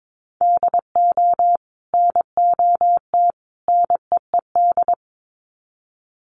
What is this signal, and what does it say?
Morse code, heard as a keyed tone, about 22 wpm, 700 Hz. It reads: DO NOT NEED
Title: Sets of 3 Words 22wpm